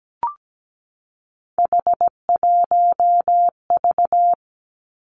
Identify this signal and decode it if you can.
Morse code, heard as a keyed tone, about 17 wpm, 700 Hz. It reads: H1V
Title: Three Challenging Letter-Number Combinations 17wpm